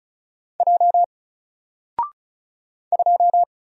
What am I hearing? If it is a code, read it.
Morse code, heard as a keyed tone, about 36 wpm, 700 Hz. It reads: J 2
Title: Single Character (Letter:Number:Punctuation:Prosign) Alt 35wpm